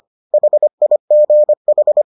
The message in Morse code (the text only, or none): HIGH